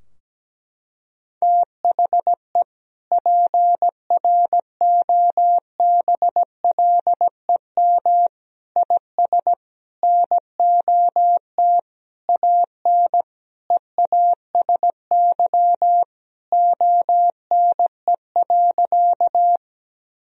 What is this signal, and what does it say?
Morse code, heard as a keyed tone, about 17 wpm, 700 Hz. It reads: THE PROBLEM IS NOT AN EASY ONE.
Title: Sentences from Top 1000 Words - Review Part 2 - 17wpm